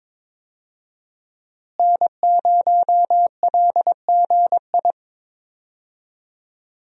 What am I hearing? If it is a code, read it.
Morse code, heard as a keyed tone, about 22 wpm, 700 Hz. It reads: N0LGI
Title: USA Call Signs - 5 Characters - Repeated 2x - 22wpm